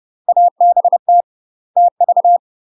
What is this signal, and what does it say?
Morse code, heard as a keyed tone, about 30 wpm, 700 Hz. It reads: ABT TV
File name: UK MASTER QSOs TRAINING / 30WPM / UK MASTER QSOs 0030